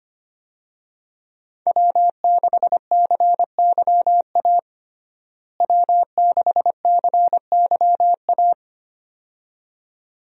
Morse code, heard as a keyed tone, about 25 wpm, 700 Hz. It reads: W6CYA W6CYA
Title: USA Call Signs - 5 Characters - Repeated 2x - 25wpm